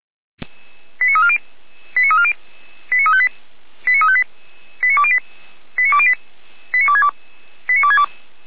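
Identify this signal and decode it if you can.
Five-tone selective call (ZVEI), heard as quick runs of five tones, tones of about 80 ms. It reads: 89340 89340 89358 89208 89208 89262 89262
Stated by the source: ZVEI five-tone call 89230 received on an SDR.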